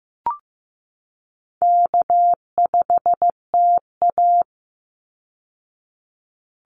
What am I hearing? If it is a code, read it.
Morse code, heard as a keyed tone, about 15 wpm, 700 Hz. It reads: K5TA
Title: Easy Call Signs Repeated 2x - 15wpm